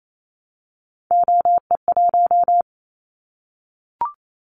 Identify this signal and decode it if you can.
Morse code, heard as a keyed tone, about 28 wpm, 700 Hz. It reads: OE1
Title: Callsign Prefixes for Top 100 DXCC - 28wpm